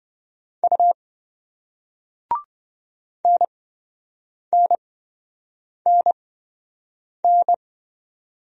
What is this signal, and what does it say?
Morse code, heard as a keyed tone, about 28 wpm, spaced as for 7 wpm, 700 Hz. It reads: U NNNN